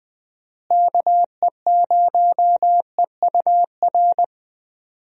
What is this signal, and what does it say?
Morse code, heard as a keyed tone, about 20 wpm, 700 Hz. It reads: KE0EUR
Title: USA Call Signs - 6 Characters - Repeated 2x - 20wpm